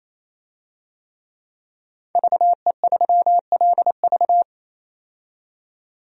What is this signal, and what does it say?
Morse code, heard as a keyed tone, about 28 wpm, 700 Hz. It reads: VE3LV
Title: DXCC-100 Call Signs - 5 Characters - 28wpm